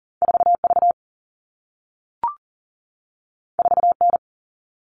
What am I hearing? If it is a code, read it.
Morse code, heard as a keyed tone, about 41 wpm, 700 Hz. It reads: 4V 4N
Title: All Cut Numbers - 2-Digit 40wpm